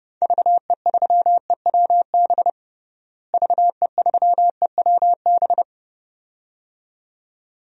Morse code, heard as a keyed tone, about 32 wpm, 700 Hz. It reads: VE3EWB VE3EWB
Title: DXCC-100 Call Signs - 6 Characters - Repeated 2x - 30wpm